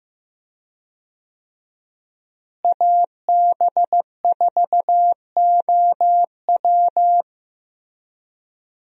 Morse code, heard as a keyed tone, about 15 wpm, 700 Hz. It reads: AB4OW
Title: USA Call Signs - 5 Characters - Repeated 3x - 15wpm